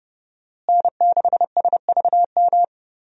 Morse code, heard as a keyed tone, about 30 wpm, 700 Hz. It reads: N6SVM
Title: USA Call Signs - 5 Characters - Repeated 3x - 30wpm